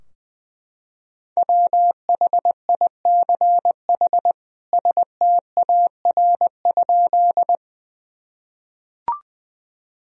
Morse code, heard as a keyed tone, about 20 wpm, 700 Hz. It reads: WHICH STAR?
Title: Sentences from Top 1000 Words - Review Part 2 - 20wpm